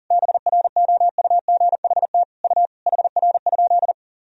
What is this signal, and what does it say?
Morse code, heard as a keyed tone, about 40 wpm, 700 Hz. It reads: BROUGHT U HR?
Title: Sentences from Top 500 Words and Abbreviations 40wpm